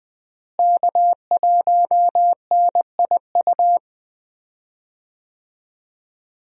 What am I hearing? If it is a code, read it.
Morse code, heard as a keyed tone, about 21 wpm, 700 Hz. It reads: K1NIU
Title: USA Call Signs - 5 Characters - Repeated 2x - 20wpm